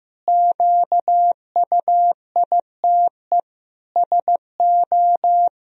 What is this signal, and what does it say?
Morse code, heard as a keyed tone, about 15 wpm, 700 Hz. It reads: QUITE SO
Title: Sentences from Top 1000 Words - Review Part 3 - 15wpm